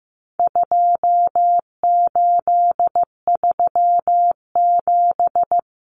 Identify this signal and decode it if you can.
Morse code, heard as a keyed tone, about 15 wpm, 700 Hz. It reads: 2837
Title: Common Cut Numbers - 4-Digit 15wpm